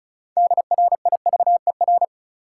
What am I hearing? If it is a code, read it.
Morse code, heard as a keyed tone, about 36 wpm, 700 Hz. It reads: DRIVER
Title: Sentences from Top 1000 Words - New Word Drill - 35wpm